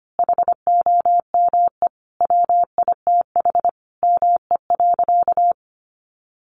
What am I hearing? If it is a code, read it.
Morse code, heard as a keyed tone, about 25 wpm, 700 Hz. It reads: HOME WITH ME.
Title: Sentences from Top 800 Words 25wpm